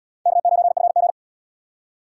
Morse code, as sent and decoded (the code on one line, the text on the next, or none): -.- ----. ..-. --.
K9FG